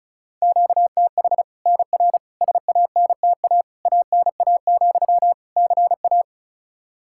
Morse code, heard as a keyed tone, about 34 wpm, 700 Hz. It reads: QTH NR SANTA ANA, CA
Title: Practice QSO - Exchange 1 - Beginning - 35wpm